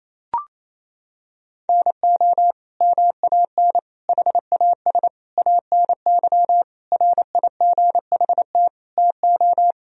Morse code, heard as a keyed tone, about 28 wpm, 700 Hz. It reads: NO MAN HAS ANY RIGHT TO
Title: Sentences from Top 1000 Words - Review Part 2 - 28wpm